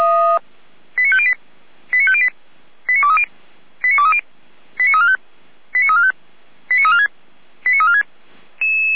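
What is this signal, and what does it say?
Five-tone selective call (ZVEI), heard as quick runs of five tones, tones of about 70 ms. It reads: 89508 89508 89230 89230 89345 89345 89346 89346
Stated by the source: Five-tone ZVEI selective call received by SDR.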